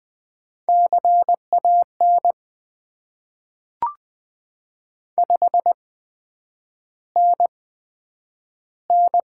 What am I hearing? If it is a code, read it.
Morse code, heard as a keyed tone, about 20 wpm, spaced as for 9 wpm, 700 Hz. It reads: CAN 5NN
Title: Top 90 QSO Elements - 8x Character Spacing - 20wpm